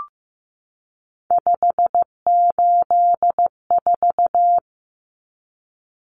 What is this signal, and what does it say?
Morse code, heard as a keyed tone, about 15 wpm, 700 Hz. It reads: 584
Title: Numbers - 3-Digit 15wpm